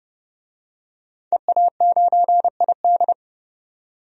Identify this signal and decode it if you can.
Morse code, heard as a keyed tone, about 30 wpm, 700 Hz. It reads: EA9ID
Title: DXCC-100 Call Signs - 5 Characters - 30wpm